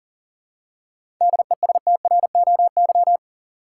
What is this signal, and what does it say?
Morse code, heard as a keyed tone, about 39 wpm, 700 Hz. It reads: DESTROY
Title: Seven Letter Words - 40wpm